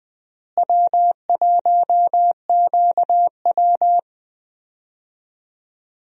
Morse code, heard as a keyed tone, about 20 wpm, 700 Hz. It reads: W1QW
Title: USA Call Signs - 4 Characters - Repeated 2x - 20wpm